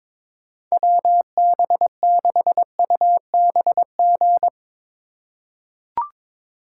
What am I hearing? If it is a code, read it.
Morse code, heard as a keyed tone, about 22 wpm, 700 Hz. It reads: WB6UBG
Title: USA Call Signs - 6 Characters - 3x Character Spacing - 22wpm